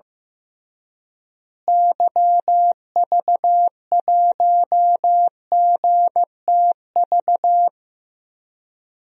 Morse code, heard as a keyed tone, about 15 wpm, 700 Hz. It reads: YV1GTV
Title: DXCC-100 Call Signs - 6 Characters - Repeated 2x - 15wpm